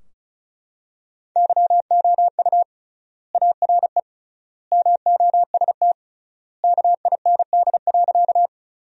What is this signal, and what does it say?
Morse code, heard as a keyed tone, about 34 wpm, 700 Hz. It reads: YOU ARE MOST KIND.
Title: 3x Word-Spacing - Sentences from Top 200 Words - 35wpm